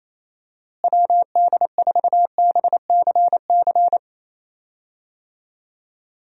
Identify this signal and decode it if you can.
Morse code, heard as a keyed tone, about 28 wpm, 700 Hz. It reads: WD4BCC
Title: USA Call Signs - 6 Characters - Repeated 3x - 28wpm